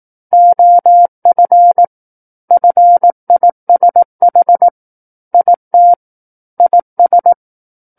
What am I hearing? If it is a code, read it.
Morse code, heard as a keyed tone, about 18 wpm, 700 Hz. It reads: OF FISH IT IS
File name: 18 WPM